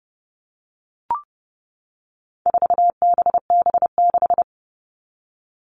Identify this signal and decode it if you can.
Morse code, heard as a keyed tone, about 30 wpm, 700 Hz. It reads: 4BB6
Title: All Cut Numbers - 4-Digit 30wpm